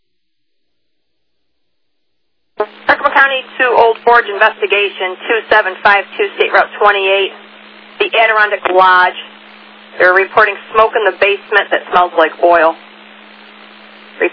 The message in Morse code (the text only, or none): none